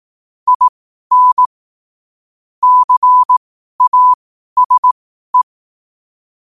Morse code, heard as a keyed tone, about 18 wpm, spaced as for 12 wpm, 1000 Hz. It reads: IN CASE
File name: qotdfiles / 12wpm